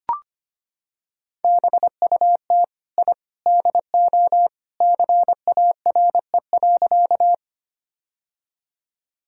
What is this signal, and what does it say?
Morse code, heard as a keyed tone, about 25 wpm, 700 Hz. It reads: BUT I DO CARE.
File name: practice / 25wpm / Sentences with 3-Word Phrases - 25wpm